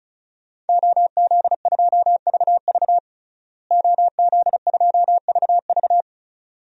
Morse code, heard as a keyed tone, about 36 wpm, 700 Hz. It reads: OZ2VV OZ2VV
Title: DXCC-100 Call Signs - 5 Characters - Repeated 2x - 35wpm